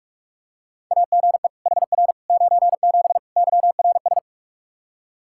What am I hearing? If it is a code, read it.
Morse code, heard as a keyed tone, about 47 wpm, 700 Hz. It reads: AGE HR 97 YRS
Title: Practice QSO - Exchange 3 - More Info - 45wpm